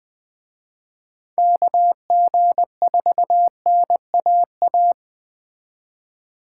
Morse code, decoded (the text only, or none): KG4NAA